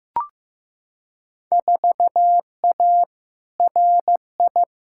Morse code, heard as a keyed tone, about 15 wpm, 700 Hz. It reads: 4A RI
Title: Field Day Exchanges Encore - 15wpm